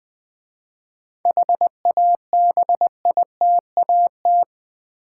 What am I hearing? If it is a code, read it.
Morse code, heard as a keyed tone, about 20 wpm, 700 Hz. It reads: HABITAT
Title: 7 Letter Words - First 3 Letters Given - 20wpm